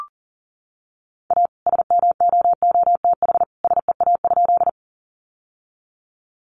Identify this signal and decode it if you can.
Morse code, heard as a keyed tone, about 40 wpm, 700 Hz. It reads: A SMOOTH SEA?
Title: Sentences from Top 1400 Words 40wpm